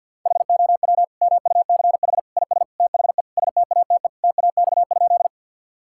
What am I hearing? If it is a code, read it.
Morse code, heard as a keyed tone, about 51 wpm, 700 Hz. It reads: HOW MUCH IS THE STATE TAX?
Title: Sentences from Top 1000 Words - Review Part 5 - 50wpm